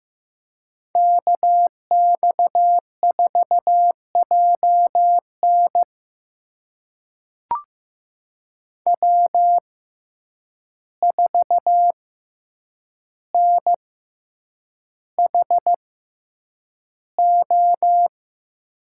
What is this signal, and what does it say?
Morse code, heard as a keyed tone, about 15 wpm, spaced as for 8 wpm, 700 Hz. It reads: KX4JN W4NHO
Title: USA Call Signs - 5 Characters - 6x Character Spacing - 15wpm